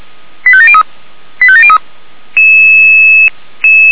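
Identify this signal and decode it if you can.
Five-tone selective call (ZVEI), heard as quick runs of five tones, tones of about 70 ms. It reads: 85703 85703
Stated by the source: ZVEI five-tone selective call received by SDR.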